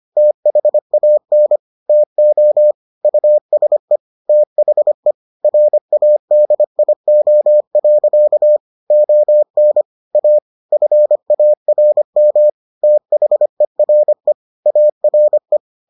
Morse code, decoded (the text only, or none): THAN TO USE THE RADIO. ON A FARM THERE ARE